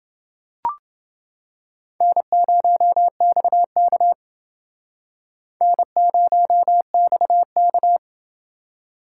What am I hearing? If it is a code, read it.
Morse code, heard as a keyed tone, about 28 wpm, 700 Hz. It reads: N0XK N0XK